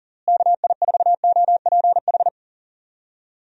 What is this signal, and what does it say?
Morse code, heard as a keyed tone, about 40 wpm, 700 Hz. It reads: KI4OPH
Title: USA Call Signs - 6 Characters - Repeated 2x - 40wpm